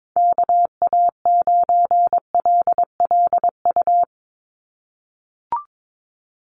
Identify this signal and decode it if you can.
Morse code, heard as a keyed tone, about 22 wpm, 700 Hz. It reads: KA9LLU
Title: USA Call Signs 22wpm